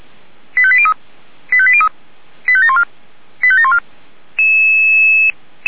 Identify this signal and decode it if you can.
Five-tone selective call (ZVEI), heard as quick runs of five tones, tones of about 70 ms. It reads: 85803 85803 85714 85714